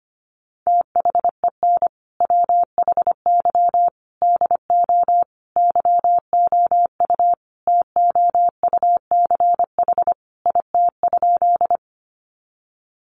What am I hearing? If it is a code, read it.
Morse code, heard as a keyed tone, about 25 wpm, 700 Hz. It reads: THEN WHY DO YOU TOUCH IT?
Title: Sentences from Top 1200 Words 25wpm